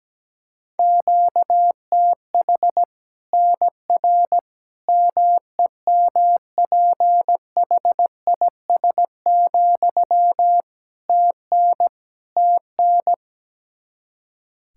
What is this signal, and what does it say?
Morse code, heard as a keyed tone, about 17 wpm, 700 Hz. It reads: QTH NR MEMPHIS, TN TN
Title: Practice QSO - Exchange 1 - Beginning - 17wpm